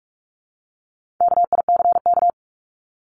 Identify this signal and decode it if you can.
Morse code, heard as a keyed tone, about 44 wpm, 700 Hz. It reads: KICK
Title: Top 1251-1500 Words 45wpm